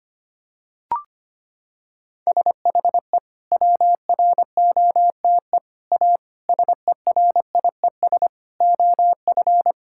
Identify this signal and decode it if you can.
Morse code, heard as a keyed tone, about 25 wpm, 700 Hz. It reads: SHE WROTE A SERIES OF
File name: practice / 25wpm / Sentences with 3-Word Phrases - 25wpm